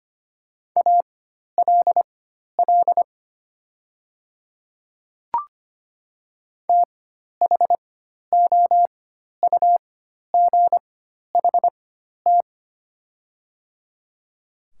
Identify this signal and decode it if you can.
Morse code, heard as a keyed tone, about 25 wpm, spaced as for 12 wpm, 700 Hz. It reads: ALL THOUGHT
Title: 4x Character Spacing - Top 300 Words - 25wpm